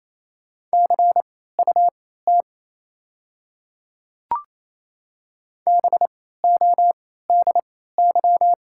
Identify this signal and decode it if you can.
Morse code, heard as a keyed tone, about 28 wpm, spaced as for 16 wpm, 700 Hz. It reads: CUT BODY
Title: 3x Character Spacing - Top 500 Words - 28wpm